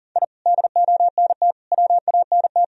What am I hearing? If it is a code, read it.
Morse code, heard as a keyed tone, about 40 wpm, 700 Hz. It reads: I DONT WANT
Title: Sentences from Top 1000 Words - Review Part 3 - 40wpm